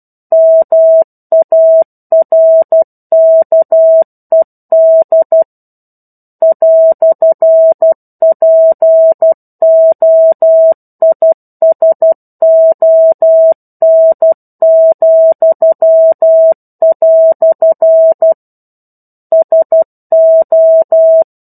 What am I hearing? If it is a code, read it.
Morse code, heard as a keyed tone, about 12 wpm, 650 Hz. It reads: MARKED "POISON," SO
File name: 01-Chapter-01 12 WPM